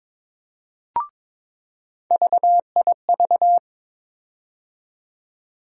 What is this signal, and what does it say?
Morse code, heard as a keyed tone, about 22 wpm, 700 Hz. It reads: VIV
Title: USA Call Sign Postfixes - 3 Letters 22wpm